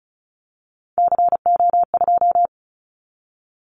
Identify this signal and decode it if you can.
Morse code, heard as a keyed tone, about 36 wpm, 700 Hz. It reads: CO2
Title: Callsign Prefixes for Top 100 DXCC - 35wpm